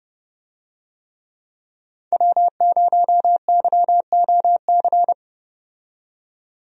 Morse code, heard as a keyed tone, about 31 wpm, 700 Hz. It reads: W0YOC
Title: USA Call Signs - 5 Characters - 30wpm